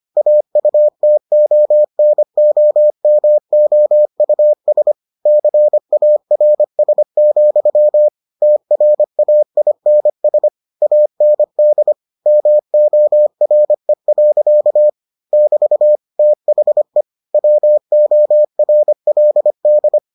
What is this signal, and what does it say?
Morse code, heard as a keyed tone, about 25 wpm, 600 Hz. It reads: AUTONOMOUS CARS, TRAINS AND MORE. <BT> THE WORLD